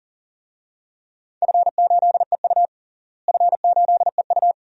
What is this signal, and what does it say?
Morse code, heard as a keyed tone, about 41 wpm, 700 Hz. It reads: F8EU F8EU
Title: DXCC-100 Call Signs - 4 Characters - Repeated 2x - 40wpm